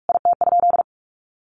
Morse code, heard as a keyed tone, about 44 wpm, 700 Hz. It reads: IT?
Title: Sentences from Top 1100 Words 45wpm